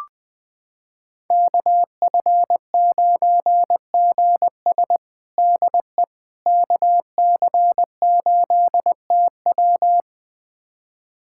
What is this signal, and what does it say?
Morse code, heard as a keyed tone, about 20 wpm, 700 Hz. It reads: KF9GS DE KC8TW
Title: USA Callsign DE Callsign - 5 Characters - 20wpm